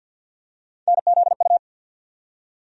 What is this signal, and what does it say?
Morse code, heard as a keyed tone, about 52 wpm, 700 Hz. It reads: NZU
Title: USA Call Sign Postfixes - 3 Letters 50wpm